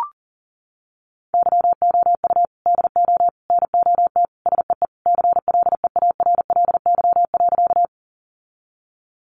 Morse code, heard as a keyed tone, about 40 wpm, 700 Hz. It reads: YOU DO NOT SEE CLEARLY.
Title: Sentences from Top 1300 Words 40wpm